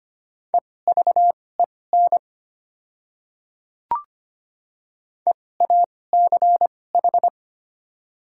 Morse code, heard as a keyed tone, about 25 wpm, spaced as for 18 wpm, 700 Hz. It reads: EVEN EACH
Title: 2x Character Spacing - Top 300 Words - 25wpm